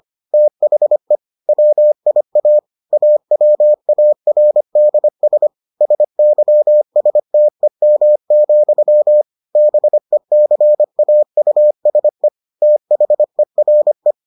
Morse code, heard as a keyed tone, about 25 wpm, 600 Hz. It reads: THE WIA AWARDS SYSTEM, BECAUSE THERE